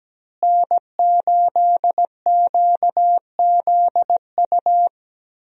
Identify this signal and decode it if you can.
Morse code, heard as a keyed tone, about 17 wpm, 700 Hz. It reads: N8QZU
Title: USA Call Signs - 5 Characters - Repeated 3x - 17wpm